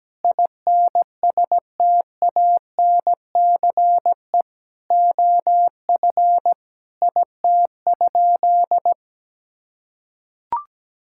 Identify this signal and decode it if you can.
Morse code, heard as a keyed tone, about 17 wpm, 700 Hz. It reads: INSTANCE OF IT?
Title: Sentences from Top 1000 Words - Review Part 5 - 17wpm